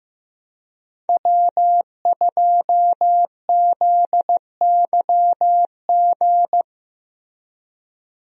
Morse code, heard as a keyed tone, about 15 wpm, 700 Hz. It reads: W2ZYG
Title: USA Call Signs - 5 Characters - Repeated 2x - 15wpm